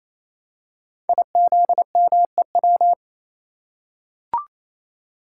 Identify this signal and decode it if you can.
Morse code, heard as a keyed tone, about 29 wpm, 700 Hz. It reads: IZMEW
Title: Code Groups - 5 Letters - 28wpm